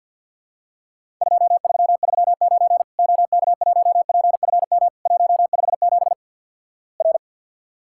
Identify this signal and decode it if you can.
Morse code, heard as a keyed tone, about 50 wpm, 700 Hz. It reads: 2339 OK1PFM 157 R
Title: Worked All Europe DX Contest - QTC Exchange with Repeats - 50wpm